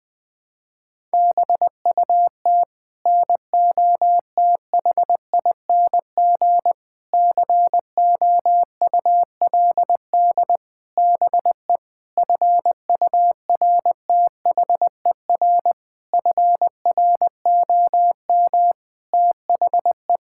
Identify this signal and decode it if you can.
Morse code, heard as a keyed tone, about 21 wpm, 700 Hz. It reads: BUT NOTHING COULD BE FURTHER FROM THE
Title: Sets of 8 Words - 20wpm